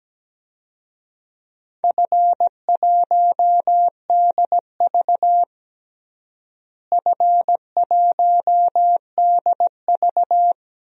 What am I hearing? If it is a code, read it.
Morse code, heard as a keyed tone, about 17 wpm, 700 Hz. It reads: F1DV F1DV
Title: DXCC-100 Call Signs - 4 Characters - Repeated 2x - 17wpm